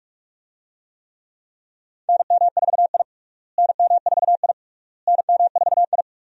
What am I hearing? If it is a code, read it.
Morse code, heard as a keyed tone, about 41 wpm, 700 Hz. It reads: NM4I NM4I NM4I
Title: USA Call Signs - 4 Characters - Repeated 3x - 45wpm